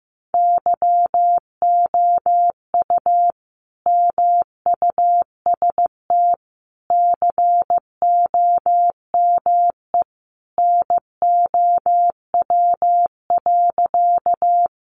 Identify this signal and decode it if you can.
Morse code, heard as a keyed tone, about 14 wpm, 700 Hz. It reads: YOU MUST COME NOW.
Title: Sentences from Top 200 Words 15wpm